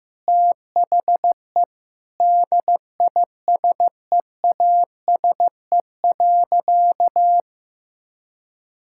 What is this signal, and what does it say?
Morse code, heard as a keyed tone, about 15 wpm, 700 Hz. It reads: THE DISEASE.
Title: Sentences from Top 1000 Words - Review Part 6 - 15wpm